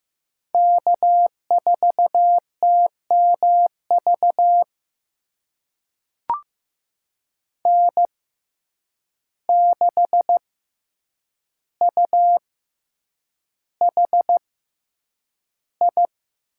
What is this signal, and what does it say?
Morse code, heard as a keyed tone, about 15 wpm, spaced as for 8 wpm, 700 Hz. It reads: K4TMV N6UHI
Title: USA Call Signs - 5 Characters - 6x Character Spacing - 15wpm